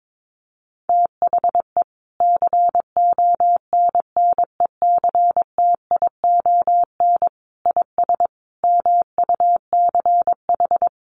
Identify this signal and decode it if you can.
Morse code, heard as a keyed tone, about 22 wpm, 700 Hz. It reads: THE CONNECTION IS MUCH
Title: Sentences from Top 1300 Words 22wpm